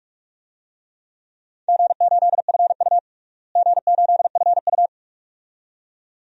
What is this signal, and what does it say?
Morse code, heard as a keyed tone, about 44 wpm, 700 Hz. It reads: G8FU G8FU
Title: DXCC-100 Call Signs - 4 Characters - Repeated 2x - 45wpm